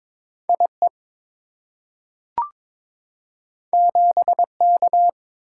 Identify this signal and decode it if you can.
Morse code, heard as a keyed tone, about 22 wpm, 690 Hz. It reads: IE 7K
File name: practice / 22wpm / Two Letter-Number Combinations - 22wpm